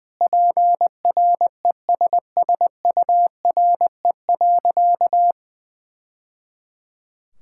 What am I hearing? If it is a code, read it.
Morse code, heard as a keyed tone, about 20 wpm, 700 Hz. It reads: PRESSURE.
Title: Sentences from Top 1000 Words - Review Part 6 - 20wpm